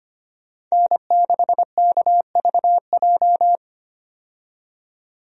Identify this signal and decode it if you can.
Morse code, heard as a keyed tone, about 25 wpm, 700 Hz. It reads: N6KVJ
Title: USA Call Signs - 5 Characters - 25wpm